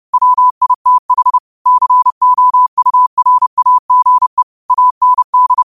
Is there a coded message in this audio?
Yes: Morse code from 0.1 to 5.6 s, heard as a keyed tone, about 30 wpm, 1000 Hz. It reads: WITH COURAGE AND